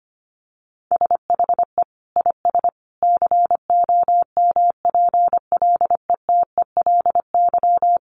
Morse code, heard as a keyed tone, about 25 wpm, 700 Hz. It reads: SHE IS COMPLETELY
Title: Additional Easy Phrases 25wpm